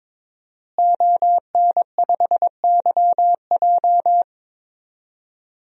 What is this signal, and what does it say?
Morse code, heard as a keyed tone, about 22 wpm, 700 Hz. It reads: ON5YJ
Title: DXCC-100 Call Signs - 5 Characters - Repeated 3x - 22wpm